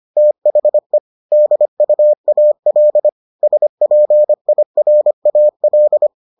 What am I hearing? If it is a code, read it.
Morse code, heard as a keyed tone, about 25 wpm, 600 Hz. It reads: THE DUAL SPIRAL